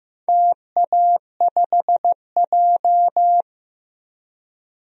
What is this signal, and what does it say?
Morse code, heard as a keyed tone, about 15 wpm, 700 Hz. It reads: TA5J
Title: DXCC-100 Call Signs - 4 Characters - Repeated 3x - 15wpm